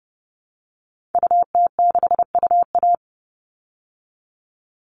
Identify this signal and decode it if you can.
Morse code, heard as a keyed tone, about 30 wpm, 700 Hz. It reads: UT6UA
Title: Call Signs Repeated 2x - 30wpm